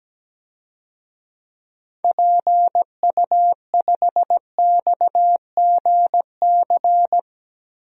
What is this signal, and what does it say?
Morse code, heard as a keyed tone, about 17 wpm, 700 Hz. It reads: PU5XGC
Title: DXCC-100 Call Signs - 6 Characters - 17wpm